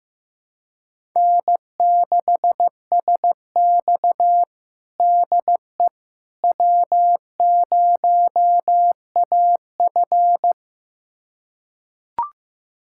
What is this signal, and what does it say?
Morse code, heard as a keyed tone, about 15 wpm, 700 Hz. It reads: N6SX DE W0AF
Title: USA Callsign DE Callsign - 4 Characters - 15wpm